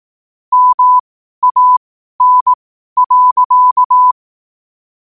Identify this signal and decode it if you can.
Morse code, heard as a keyed tone, about 18 wpm, spaced as for 12 wpm, 1000 Hz. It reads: MAN.
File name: qotdfiles / 12wpm